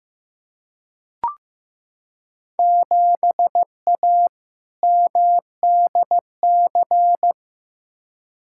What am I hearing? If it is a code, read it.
Morse code, heard as a keyed tone, about 15 wpm, 700 Hz. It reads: 7A MDC
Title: Field Day Exchanges Encore - 15wpm